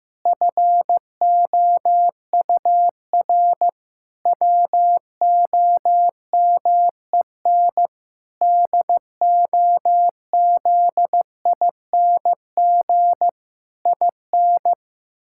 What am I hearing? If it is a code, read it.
Morse code, heard as a keyed tone, about 16 wpm, 700 Hz. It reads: FOUR WOMEN DOZING IN
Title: Pangram Challenge v2 - 15wpm